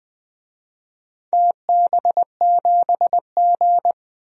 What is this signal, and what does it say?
Morse code, heard as a keyed tone, about 20 wpm, 700 Hz. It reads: TB7G
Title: DXCC-100 Call Signs - 4 Characters - 20wpm